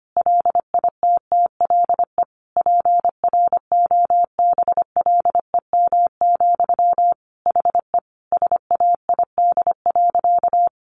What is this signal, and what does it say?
Morse code, heard as a keyed tone, about 25 wpm, 700 Hz. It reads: LITTLE PROBLEM, HE SAID.
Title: Sentences from Top 1000 Words 25wpm